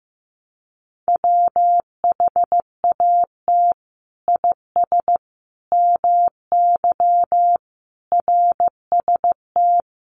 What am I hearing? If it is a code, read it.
Morse code, heard as a keyed tone, about 14 wpm, 700 Hz. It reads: WHAT IS MY RST
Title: Common QSO Phrases 15wpm